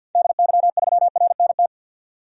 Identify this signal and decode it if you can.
Morse code, heard as a keyed tone, about 46 wpm, 700 Hz. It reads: DY3RNT
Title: DXCC-100 Call Signs - 6 Characters - 50wpm